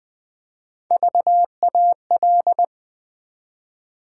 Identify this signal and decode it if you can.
Morse code, heard as a keyed tone, about 20 wpm, 700 Hz. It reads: VAL
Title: Build A Word - 1 Letter At A Time - 3 to 5-Letter Words - 20wpm